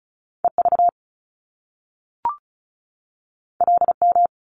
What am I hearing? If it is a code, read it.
Morse code, heard as a keyed tone, about 36 wpm, 700 Hz. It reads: EV LM
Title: Top Two Letter Combinations in Words 35wpm